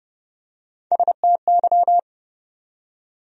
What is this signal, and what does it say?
Morse code, heard as a keyed tone, about 30 wpm, 700 Hz. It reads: STY
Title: USA Call Sign Postfixes - 3 Letters 30wpm